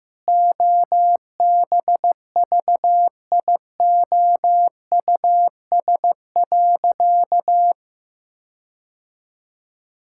Morse code, heard as a keyed tone, about 15 wpm, 700 Hz. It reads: OBVIOUS.
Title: Sentences from Top 1600 Words 15wpm